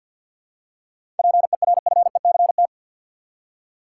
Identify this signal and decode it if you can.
Morse code, heard as a keyed tone, about 47 wpm, 700 Hz. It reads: PERFECT
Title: Seven Letter Words - 50wpm